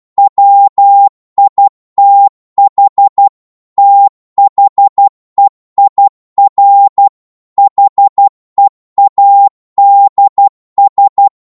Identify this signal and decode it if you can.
Morse code, heard as a keyed tone, about 12 wpm, 800 Hz. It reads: WITH THEIR HEADS
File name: Morse / psg-12wpm